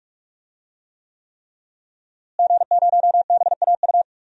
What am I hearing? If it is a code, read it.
Morse code, heard as a keyed tone, about 46 wpm, 700 Hz. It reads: G0BAU
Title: DXCC-100 Call Signs - 5 Characters - Repeated 3x - 45wpm